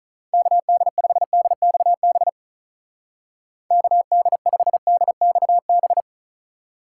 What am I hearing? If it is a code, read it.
Morse code, heard as a keyed tone, about 44 wpm, 700 Hz. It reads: KD5DXB KD5DXB